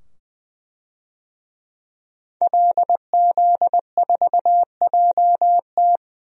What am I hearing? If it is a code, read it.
Morse code, heard as a keyed tone, about 20 wpm, 700 Hz. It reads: LZ4JT
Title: DXCC-100 Call Signs - 5 Characters - 20wpm